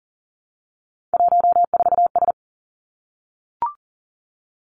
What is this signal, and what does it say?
Morse code, heard as a keyed tone, about 41 wpm, 700 Hz. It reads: 14S
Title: Three Challenging Letter-Number Combinations 40wpm